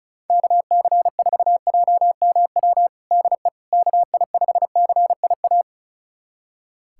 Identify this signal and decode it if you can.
Morse code, heard as a keyed tone, about 36 wpm, 700 Hz. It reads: KC4JMW DE KI5CIA
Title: USA Callsign DE Callsign 35wpm